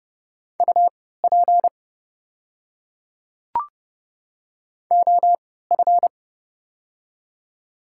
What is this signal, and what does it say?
Morse code, heard as a keyed tone, about 29 wpm, spaced as for 17 wpm, 700 Hz. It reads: UP OF
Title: 3x Character Spacing - Top 100 Words - 30wpm